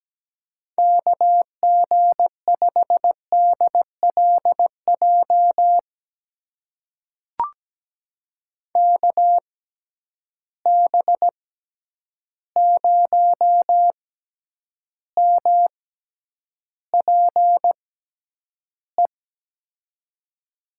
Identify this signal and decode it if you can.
Morse code, heard as a keyed tone, about 17 wpm, spaced as for 9 wpm, 700 Hz. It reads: KG5DLJ KB0MPE
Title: USA Call Signs - 6 Characters - 6x Character Spacing - 17wpm